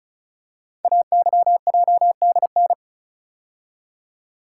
Morse code, heard as a keyed tone, about 35 wpm, 690 Hz. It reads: AYJDN
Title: Code Groups - 5 Letters - 35wpm